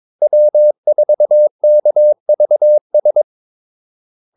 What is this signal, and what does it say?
Morse code, heard as a keyed tone, about 22 wpm, 600 Hz. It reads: W4KVS
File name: Callsigns-22wpm